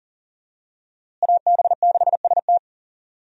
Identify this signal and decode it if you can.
Morse code, heard as a keyed tone, about 40 wpm, 700 Hz. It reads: AB6ST
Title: USA Call Signs - 5 Characters - Repeated 3x - 40wpm